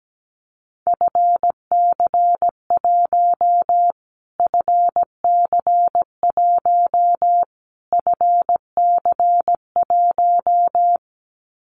Morse code, heard as a keyed tone, about 17 wpm, 700 Hz. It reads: FC1 FC1 FC1
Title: Callsign Prefixes for Top 25 DXCC - Repeated 3x - 17wpm